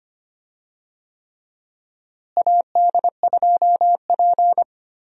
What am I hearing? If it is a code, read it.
Morse code, heard as a keyed tone, about 25 wpm, 700 Hz. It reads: AD2P
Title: USA Call Signs - 4 Characters - Repeated 3x - 25wpm